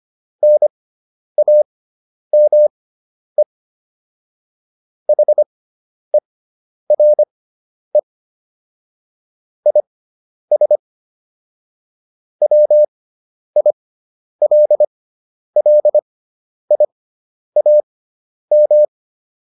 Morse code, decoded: NAME HERE IS WILLIAM

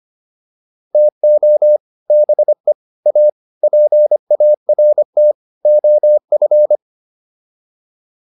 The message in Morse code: TO BE A PART OF

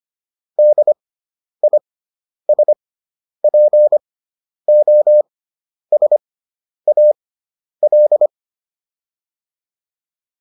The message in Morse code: DISPOSAL

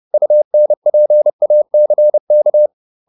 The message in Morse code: UNPACK

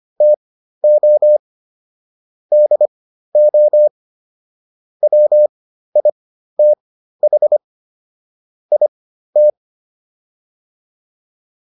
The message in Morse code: TO DO WITH IT

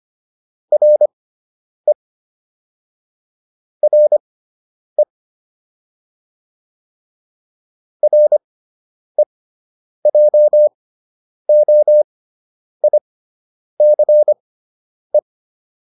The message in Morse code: RE RE REJOICE

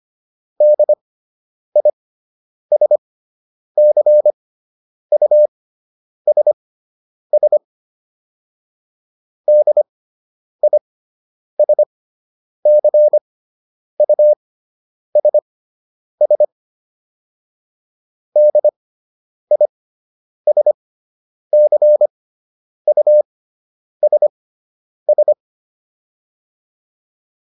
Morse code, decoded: DISCUSS DISCUSS DISCUSS

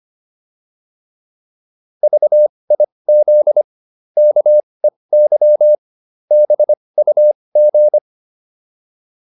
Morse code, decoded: VIZ KEY BUG

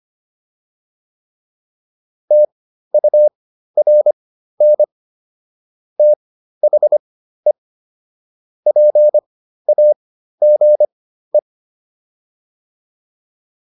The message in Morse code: TURN THE PAGE

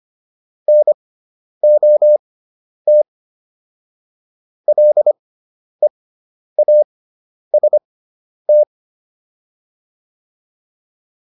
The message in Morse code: NOT LEAST

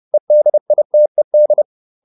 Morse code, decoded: EDITED

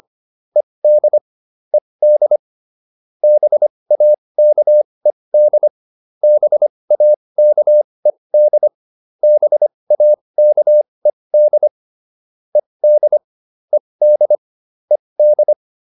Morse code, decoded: ED ED BAKED BAKED BAKED ED ED ED

